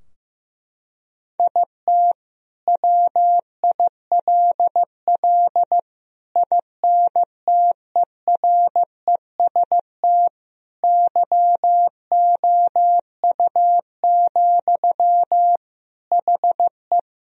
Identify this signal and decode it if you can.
Morse code, heard as a keyed tone, about 15 wpm, 700 Hz. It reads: IT WILL INTEREST YOU, HE
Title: Sentences from Top 1000 Words - Review Part 2 - 15wpm